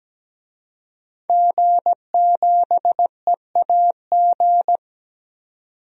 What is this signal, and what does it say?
Morse code, heard as a keyed tone, about 17 wpm, 700 Hz. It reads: G7EAG
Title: DXCC-100 Call Signs - 5 Characters - 17wpm